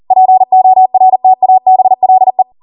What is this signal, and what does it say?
Morse code, heard as a keyed tone, about 40 wpm, 750 Hz. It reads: PORTABLE